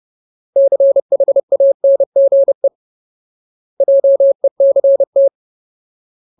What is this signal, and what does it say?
Morse code, heard as a keyed tone, about 30 wpm, 550 Hz. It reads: CHANGE JECT
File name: Ultimate Word List / 30WPM Ultimate